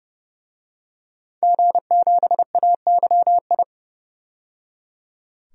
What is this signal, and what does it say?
Morse code, heard as a keyed tone, about 31 wpm, 700 Hz. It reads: G7AYI
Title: DXCC-100 Call Signs - 5 Characters - 30wpm